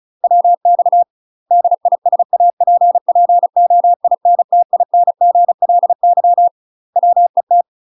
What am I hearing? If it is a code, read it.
Morse code, heard as a keyed tone, about 34 wpm, 700 Hz. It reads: WX DISAPPOINTINGLY WET
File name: UK MASTER QSOs TRAINING/35WPM